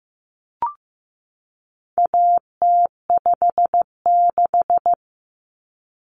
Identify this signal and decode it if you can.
Morse code, heard as a keyed tone, about 15 wpm, 700 Hz. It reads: AT56
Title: Common Cut Numbers - 4-Digit 15wpm